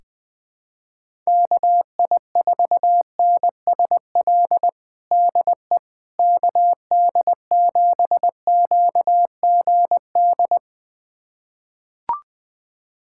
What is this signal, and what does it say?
Morse code, heard as a keyed tone, about 20 wpm, 700 Hz. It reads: KI4NSL DE KD7QGD
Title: USA Callsign DE Callsign - 6 Characters - 20wpm